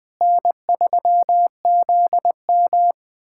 Morse code, heard as a keyed tone, about 20 wpm, 700 Hz. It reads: N3ZM
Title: USA Callsign DE Callsign - 4 Characters - 20wpm